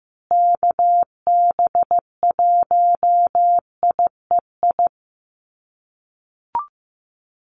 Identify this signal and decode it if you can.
Morse code, heard as a keyed tone, about 15 wpm, 700 Hz. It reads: KB1IEI
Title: USA Call Signs Repeated 2x 15wpm